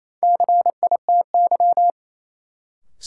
Morse code, heard as a keyed tone, about 28 wpm, 700 Hz. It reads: CITY
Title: Sentences from Top 300 Words - New Word Drill - 28wpm